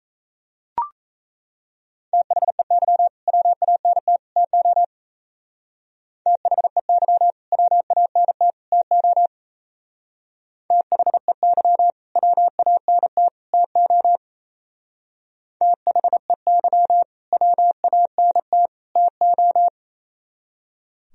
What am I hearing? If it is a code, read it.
Morse code, heard as a keyed tone, about 34 wpm, 700 Hz. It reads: THEY WANT TO THEY WANT TO THEY WANT TO THEY WANT TO
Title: Speed-Racing - Sets of 3 Words 28wpm